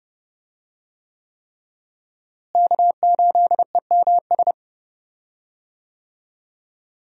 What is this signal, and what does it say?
Morse code, heard as a keyed tone, about 30 wpm, 700 Hz. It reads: K8EMS
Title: USA Call Signs - 5 Characters - Repeated 3x - 30wpm